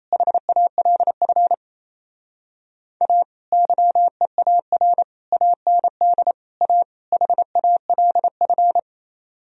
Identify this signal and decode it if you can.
Morse code, heard as a keyed tone, about 37 wpm, 700 Hz. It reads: HALF A YEAR AND A HALF